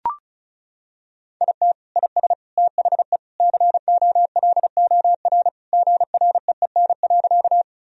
Morse code, heard as a keyed tone, about 34 wpm, 700 Hz. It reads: IT IS THE COLOR GREEN.
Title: Sentences from Top 1000 Words - Review Part 2 - 35wpm